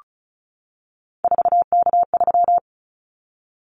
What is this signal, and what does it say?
Morse code, heard as a keyed tone, about 36 wpm, 700 Hz. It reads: 4K3
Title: Callsign Prefixes for Top 100 DXCC - 35wpm